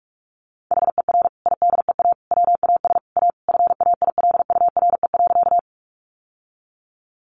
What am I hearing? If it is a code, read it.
Morse code, heard as a keyed tone, about 44 wpm, 700 Hz. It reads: HER IDEA WAS A FAILURE.
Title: Sentences from Top 900 Words 45wpm